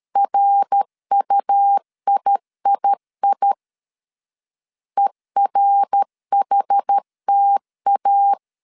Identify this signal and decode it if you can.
Morse code, heard as a keyed tone, about 13 wpm, 800 Hz. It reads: RUIII ERHTA
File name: CW / DET / 1 3 1500C S B 12wpm 2828